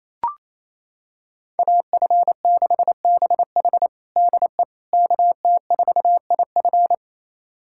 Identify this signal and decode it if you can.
Morse code, heard as a keyed tone, about 28 wpm, 700 Hz. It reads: AF6BH DE KT4IF
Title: USA Callsign DE Callsign - 5 Characters - 28wpm